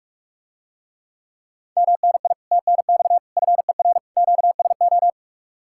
Morse code, heard as a keyed tone, about 45 wpm, 700 Hz. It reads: MNI TNX FER QSO